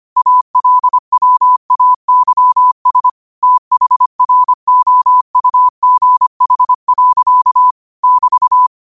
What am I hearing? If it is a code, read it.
Morse code, heard as a keyed tone, about 25 wpm, 1000 Hz. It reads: ALWAYS THROUGH. <BT>
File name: qotdfiles / 25wpm